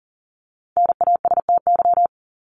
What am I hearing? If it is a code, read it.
Morse code, heard as a keyed tone, about 40 wpm, 700 Hz. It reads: NASTY
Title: Top 1751-2000 Words 40wpm